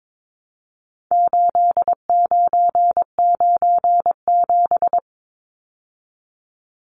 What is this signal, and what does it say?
Morse code, heard as a keyed tone, about 22 wpm, 700 Hz. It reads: 8997
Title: Numbers - 4-Digit 22wpm